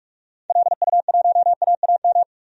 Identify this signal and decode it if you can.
Morse code, heard as a keyed tone, about 46 wpm, 700 Hz. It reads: LU1AAM
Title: DXCC-100 Call Signs - 6 Characters - 45wpm